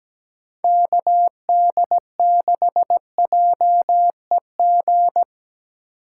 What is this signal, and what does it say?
Morse code, heard as a keyed tone, about 17 wpm, 700 Hz. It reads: KD6JEG
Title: USA Call Signs - 6 Characters - 3x Character Spacing - 17wpm